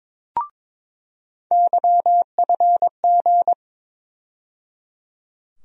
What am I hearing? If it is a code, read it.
Morse code, heard as a keyed tone, about 21 wpm, 700 Hz. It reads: YFG
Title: USA Call Sign Postfixes - 3 Letters 22wpm